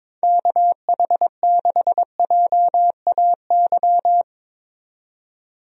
Morse code, heard as a keyed tone, about 22 wpm, 700 Hz. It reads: KH6JAY
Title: DXCC-100 Call Signs - 6 Characters - 22wpm